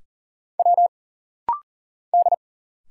Morse code, heard as a keyed tone, about 40 wpm, 700 Hz. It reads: W D
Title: Single Letters - Rapid-Fire 40wpm